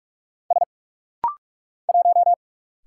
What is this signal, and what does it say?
Morse code, heard as a keyed tone, about 45 wpm, 700 Hz. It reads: S 1
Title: Single Letter-Number - Rapid-Fire 45wpm